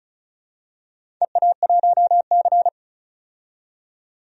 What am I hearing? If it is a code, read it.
Morse code, heard as a keyed tone, about 36 wpm, 700 Hz. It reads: EA1C
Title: DXCC-100 Call Signs - 4 Characters - 35wpm